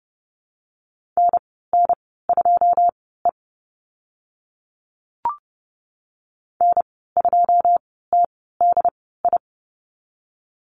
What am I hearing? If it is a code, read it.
Morse code, heard as a keyed tone, about 30 wpm, spaced as for 17 wpm, 700 Hz. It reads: NN2E N2TDI
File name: Lesson 14g - Number 2 - Call Signs 30wpm